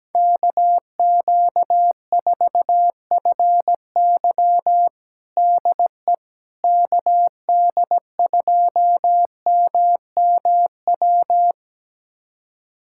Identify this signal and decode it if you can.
Morse code, heard as a keyed tone, about 17 wpm, 700 Hz. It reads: KQ4FY DE KD2MMW
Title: USA Callsign DE Callsign 17wpm